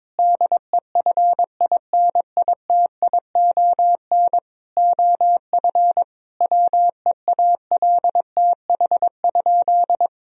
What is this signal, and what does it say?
Morse code, heard as a keyed tone, about 22 wpm, 700 Hz. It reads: DEFINITION OF WEALTH?
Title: Sentences from Top 1000 Words - Review Part 4 - 22wpm